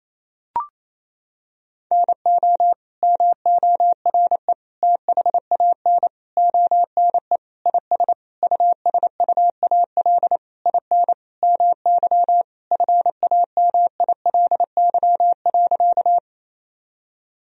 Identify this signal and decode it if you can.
Morse code, heard as a keyed tone, about 28 wpm, 700 Hz. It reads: NO MORE THAN ONE IS USUAL IN MY FAMILY.
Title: Sentences from Top 1000 Words - Review Part 2 - 28wpm